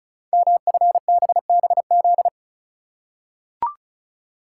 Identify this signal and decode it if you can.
Morse code, heard as a keyed tone, about 36 wpm, 700 Hz. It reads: MFBBZ
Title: Code Groups - 5 Letters - 35wpm